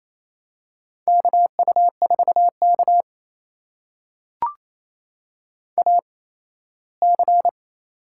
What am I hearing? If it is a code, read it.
Morse code, heard as a keyed tone, about 28 wpm, spaced as for 17 wpm, 700 Hz. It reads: KU4K AC
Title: USA Call Signs - 4 Characters - 8x Character Spacing - 28wpm